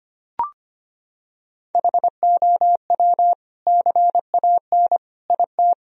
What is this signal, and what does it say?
Morse code, heard as a keyed tone, about 25 wpm, 700 Hz. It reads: HOW CAN IT
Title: Sentences from Top 1000 Words - Review Part 2 - 25wpm